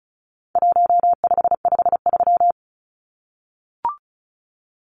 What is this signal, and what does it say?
Morse code, heard as a keyed tone, about 34 wpm, 700 Hz. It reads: 1553
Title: Numbers - 4-Digit 35wpm